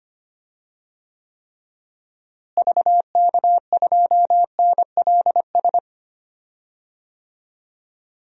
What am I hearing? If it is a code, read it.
Morse code, heard as a keyed tone, about 25 wpm, 700 Hz. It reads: VK2NLS
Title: DXCC-100 Call Signs - 6 Characters - Repeated 3x - 25wpm